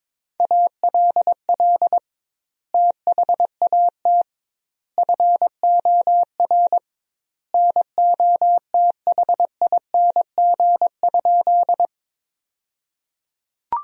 Morse code, heard as a keyed tone, about 22 wpm, 700 Hz. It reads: ALL THAT FOR NOTHING?
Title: 2x Word-Spacing - Sentences from Top 500 Words - 22wpm